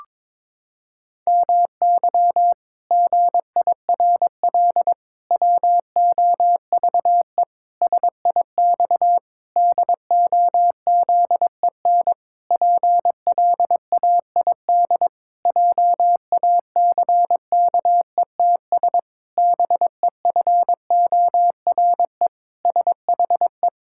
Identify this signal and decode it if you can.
Morse code, heard as a keyed tone, about 22 wpm, 700 Hz. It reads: MY GIRL WOVE SIX DOZEN PLAID JACKETS BEFORE SHE
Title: Pangram Challenge v2 - 22wpm